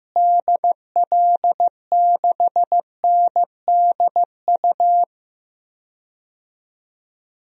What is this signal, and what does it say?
Morse code, heard as a keyed tone, about 15 wpm, 700 Hz. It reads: DL6NDU
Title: DXCC-100 Call Signs - 6 Characters - Repeated 2x - 15wpm